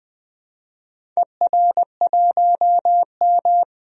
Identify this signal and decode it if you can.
Morse code, heard as a keyed tone, about 20 wpm, 700 Hz. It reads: ER1M
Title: DXCC-100 Call Signs - 4 Characters - 20wpm